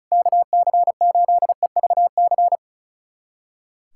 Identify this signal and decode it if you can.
Morse code, heard as a keyed tone, about 36 wpm, 700 Hz. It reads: KC8EVC
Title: USA Callsign DE Callsign 35wpm